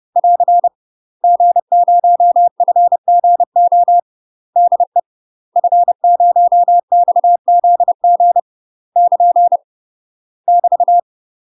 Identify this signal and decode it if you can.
Morse code, heard as a keyed tone, about 30 wpm, 700 Hz. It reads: <AR> G0FGO DE F0XZG <KN> <BT>
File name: UK MASTER QSOs TRAINING / 30WPM / UK MASTER QSOs 0041